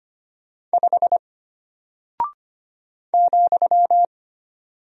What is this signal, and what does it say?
Morse code, heard as a keyed tone, about 25 wpm, 700 Hz. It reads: 5 ,
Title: Single Character (Letter:Number:Punctuation:Prosign) Alt 25wpm